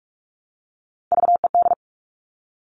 Morse code, heard as a keyed tone, about 46 wpm, 690 Hz. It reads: VED
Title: Top Three Letter Combinations in Words 45wpm